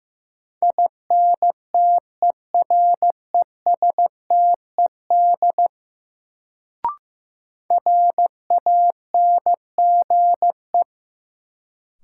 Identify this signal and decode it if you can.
Morse code, heard as a keyed tone, about 15 wpm, 700 Hz. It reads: INTERESTED RANGE
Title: Sentences from Top 900 Words - New Word Drill - 15wpm